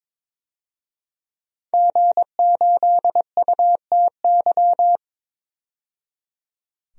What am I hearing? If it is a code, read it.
Morse code, heard as a keyed tone, about 21 wpm, 700 Hz. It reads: G8UTY
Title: DXCC-100 Call Signs - 5 Characters - 22wpm